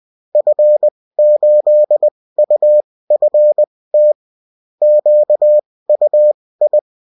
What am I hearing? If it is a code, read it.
Morse code, heard as a keyed tone, about 20 wpm, spaced as for 16 wpm, 600 Hz. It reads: F8UFT QUI
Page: bulletin-uft-sous-forme-texte-et-en-cw